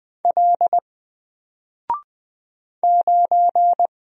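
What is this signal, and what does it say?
Morse code, heard as a keyed tone, about 20 wpm, 700 Hz. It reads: L 9
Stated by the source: Single letter and number Morse drill at 20 wpm, from N4LCD.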